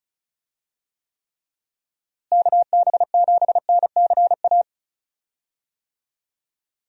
Morse code, heard as a keyed tone, about 34 wpm, 700 Hz. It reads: KB7NCA
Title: USA Call Signs - 6 Characters - Repeated 2x - 35wpm